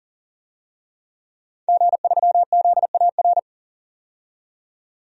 Morse code, heard as a keyed tone, about 39 wpm, 700 Hz. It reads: G3ZAR
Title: DXCC-100 Call Signs - 5 Characters - 40wpm